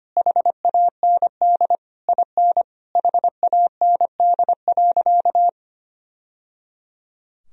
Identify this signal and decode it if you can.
Morse code, heard as a keyed tone, about 25 wpm, 700 Hz. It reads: HAND IN HAND.
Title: Sentences from Top 1000 Words - Review Part 4 - 25wpm